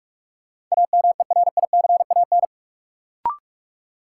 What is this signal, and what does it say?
Morse code, heard as a keyed tone, about 46 wpm, 700 Hz. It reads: AMERICAN
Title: Sets of 2 Words - New Word Drill - 45wpm